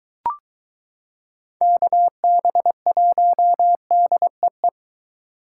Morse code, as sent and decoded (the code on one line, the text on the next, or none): -.- -... .---- -.. .
KB1DEE